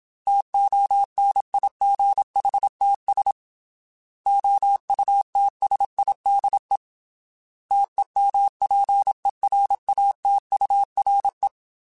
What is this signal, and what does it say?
Morse code, heard as a keyed tone, about 27 wpm, 800 Hz. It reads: TONIGHTS OUTSIDE TEMPERATURE
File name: Weather Sets / Weather Sets-mp3s / Weather 4 10 @25